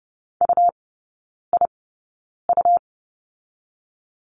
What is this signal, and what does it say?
Morse code, heard as a keyed tone, about 30 wpm, spaced as for 9 wpm, 700 Hz. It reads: UIU
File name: Lesson 17e - Letter u - Three Characters 30wpm